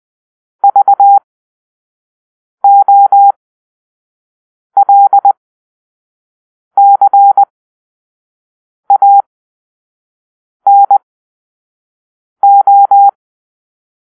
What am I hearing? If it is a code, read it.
Morse code, heard as a keyed tone, about 20 wpm, spaced as for 5 wpm, 800 Hz. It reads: VOLCANO
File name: arrl / Grp-5 Drill-4